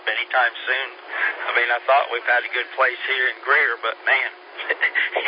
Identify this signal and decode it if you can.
Emergency Alert System (SAME) tones, heard as harsh buzzing bursts, with speech. It is heard from 0.4 to 4.4 s.